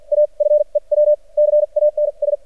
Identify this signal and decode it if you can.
Morse code, heard as a keyed tone, about 50 wpm, spaced as for 40 wpm, 600 Hz. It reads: A FEW CANS
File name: archived / CRSnet